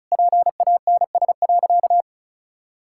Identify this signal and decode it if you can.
Morse code, heard as a keyed tone, about 36 wpm, 700 Hz. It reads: PANS.
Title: Pangram Challenge v2 - 35wpm